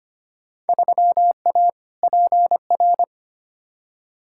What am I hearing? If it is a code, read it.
Morse code, heard as a keyed tone, about 25 wpm, 700 Hz. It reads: 3A PR
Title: Field Day Exchanges Encore - 25wpm